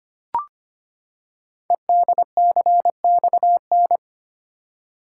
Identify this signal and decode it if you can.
Morse code, heard as a keyed tone, about 25 wpm, 690 Hz. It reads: EDCXN
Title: Code Groups - 5 Letters - 25wpm